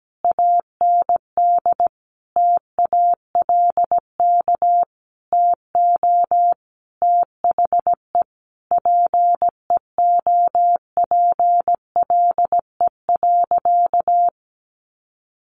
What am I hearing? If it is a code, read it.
Morse code, heard as a keyed tone, about 17 wpm, 700 Hz. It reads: AND TALK TO THE PEOPLE.
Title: Sentences from Top 400 Words 17wpm